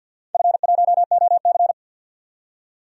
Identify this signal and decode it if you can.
Morse code, heard as a keyed tone, about 51 wpm, 700 Hz. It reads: F1OC
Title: DXCC-100 Call Signs - 4 Characters - 50wpm